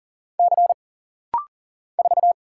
Morse code, heard as a keyed tone, about 40 wpm, 700 Hz. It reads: C 4
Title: Single Letter-Number Alt - Rapid-Fire 40wpm